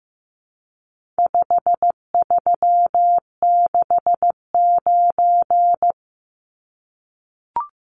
Morse code, heard as a keyed tone, about 14 wpm, 700 Hz. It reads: 5369
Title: Numbers - 4-Digit 15wpm